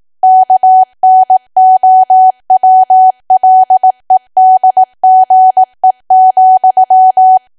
Morse code, heard as a keyed tone, about 18 wpm, 750 Hz. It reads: KNOWLEDGE,
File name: cwa / W1AW-Level2 / 18WPM